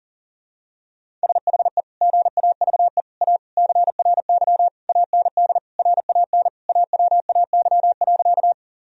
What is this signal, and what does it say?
Morse code, heard as a keyed tone, about 40 wpm, 700 Hz. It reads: SHE GAVE A CRY AND RAN AWAY.
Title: Sentences from Top 1000 Words - Review Part 3 - 40wpm